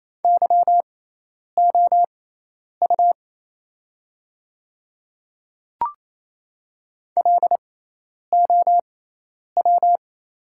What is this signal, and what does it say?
Morse code, heard as a keyed tone, about 28 wpm, spaced as for 10 wpm, 690 Hz. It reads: YOU LOW
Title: 6x Character Spacing - Top 300 Words - 28wpm